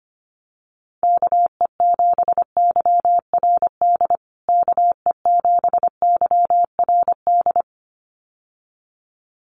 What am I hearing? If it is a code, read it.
Morse code, heard as a keyed tone, about 25 wpm, 700 Hz. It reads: KE7YRD KE7YRD
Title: USA Call Signs Repeated 2x 25wpm